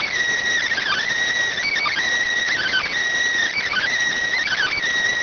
Emergency Alert System (SAME) tones, heard as harsh buzzing bursts. It is heard from 0.7 to 4.7 s.